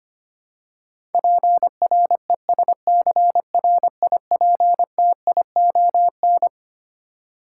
Morse code, heard as a keyed tone, about 25 wpm, 700 Hz. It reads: PRESCRIPTION
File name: practice / 25wpm / Twelve Letter Words - 25wpm